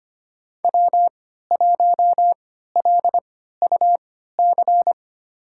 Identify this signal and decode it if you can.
Morse code, heard as a keyed tone, about 25 wpm, spaced as for 14 wpm, 700 Hz. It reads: W1LUC
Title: USA Call Signs - 5 Characters - 3x Character Spacing - 25wpm